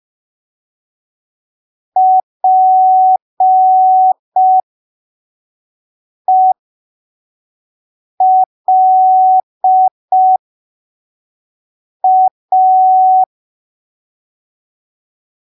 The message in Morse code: PELA